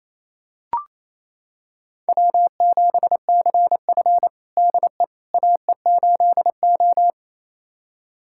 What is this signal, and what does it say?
Morse code, heard as a keyed tone, about 28 wpm, 690 Hz. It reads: W7CF DE AE8O
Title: USA Callsign DE Callsign - 4 Characters - 28wpm